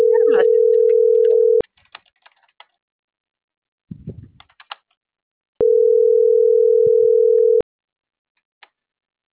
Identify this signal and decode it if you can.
Ringback tone (North America), 440 + 480 Hz.